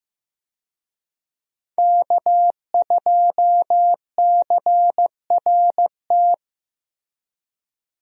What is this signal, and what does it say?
Morse code, heard as a keyed tone, about 15 wpm, 700 Hz. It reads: K2CRT
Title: USA Call Signs - 5 Characters - Repeated 3x - 15wpm